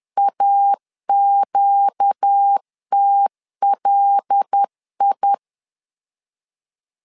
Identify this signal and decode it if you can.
Morse code, heard as a keyed tone, about 10 wpm, 800 Hz. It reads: AQTLI
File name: LEZIONI / 5 LBVC / 5 L 10wpm 2259